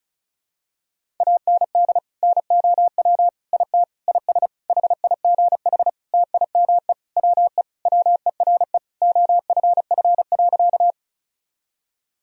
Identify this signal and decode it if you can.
Morse code, heard as a keyed tone, about 34 wpm, 700 Hz. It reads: AND NOW IT IS HIGH TIME WE WERE OFF.
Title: Sentences from Top 1000 Words - Review Part 1 - 35wpm